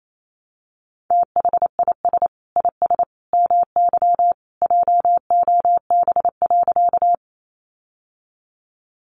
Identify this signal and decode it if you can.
Morse code, heard as a keyed tone, about 28 wpm, 700 Hz. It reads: THIS IS MY JOB.
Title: Sentences from Top 600 Words 28wpm